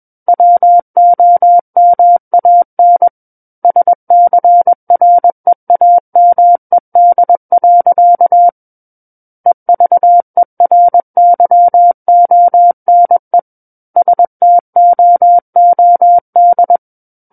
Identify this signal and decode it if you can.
Morse code, heard as a keyed tone, about 21 wpm, 700 Hz. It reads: WOMAN SCREAMED. EVERYONE STOOD
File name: Book One / 12-Chapter 12 - 21 WPM